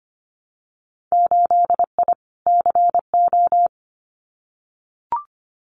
Morse code, heard as a keyed tone, about 25 wpm, 700 Hz. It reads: 8I CO
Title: Winter Field Day Exchanges Encore 25wpm